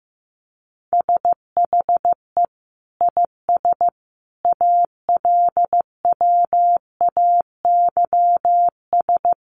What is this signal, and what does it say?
Morse code, heard as a keyed tone, about 15 wpm, 700 Hz. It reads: SHE IS ALWAYS
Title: Sentences from Top 1200 Words 15wpm